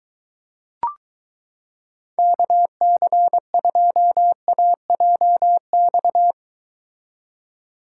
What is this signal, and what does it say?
Morse code, heard as a keyed tone, about 23 wpm, 700 Hz. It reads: KC2AJX